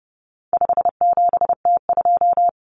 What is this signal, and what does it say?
Morse code, heard as a keyed tone, about 30 wpm, 700 Hz. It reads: 57T2
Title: Common Cut Numbers - 4-Digit 30wpm